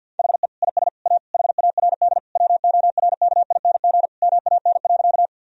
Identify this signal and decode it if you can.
Morse code, heard as a keyed tone, about 50 wpm, 700 Hz. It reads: HE IS A HARD WORKING MAN.
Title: Sentences from Top 1000 Words - Review Part 4 - 50wpm